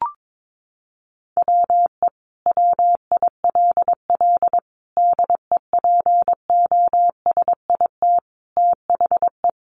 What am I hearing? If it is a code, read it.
Morse code, heard as a keyed tone, about 22 wpm, 700 Hz. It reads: WE WILL DEPOSIT THE
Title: Sentences from Top 1300 Words 22wpm